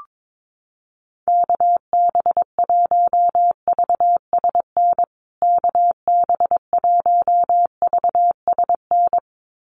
Morse code, heard as a keyed tone, about 22 wpm, 700 Hz. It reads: KB1VSN KB1VSN
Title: USA Call Signs Repeated 2x 22wpm